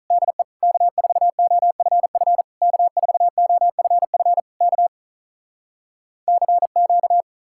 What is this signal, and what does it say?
Morse code, heard as a keyed tone, about 46 wpm, 700 Hz. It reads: DE K4OFF K4OFF K CQ